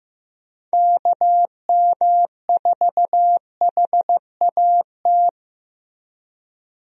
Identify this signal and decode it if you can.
Morse code, heard as a keyed tone, about 16 wpm, 700 Hz. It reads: KM4HAT
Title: USA Call Signs - 6 Characters - Repeated 2x - 15wpm